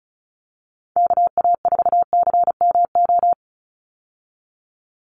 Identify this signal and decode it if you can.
Morse code, heard as a keyed tone, about 36 wpm, 700 Hz. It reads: KA4CMO
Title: USA Call Signs 35wpm